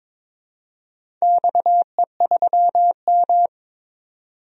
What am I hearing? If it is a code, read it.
Morse code, heard as a keyed tone, about 22 wpm, 700 Hz. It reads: XE3M
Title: DXCC-100 Call Signs - 4 Characters - 22wpm